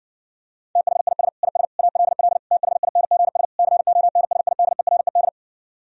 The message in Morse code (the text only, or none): THIS IS ALL THINGS CONSIDERED